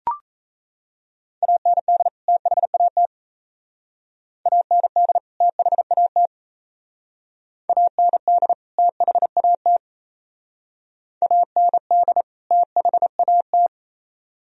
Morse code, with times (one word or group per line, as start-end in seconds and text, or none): 1.4-2.1 AND
2.3-3.1 THAT
4.5-5.2 AND
5.4-6.3 THAT
7.7-8.5 AND
8.8-9.8 THAT
11.2-12.2 AND
12.5-13.7 THAT